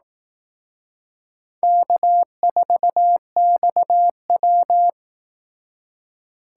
Morse code, heard as a keyed tone, about 18 wpm, 700 Hz. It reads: K4XW
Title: Speed Racing - USA 1x2 Call Signs 15wpm